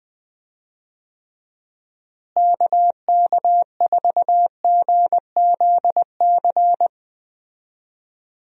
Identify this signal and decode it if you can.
Morse code, heard as a keyed tone, about 21 wpm, 700 Hz. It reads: KK4GZC
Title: USA Call Signs - 6 Characters - Repeated 2x - 20wpm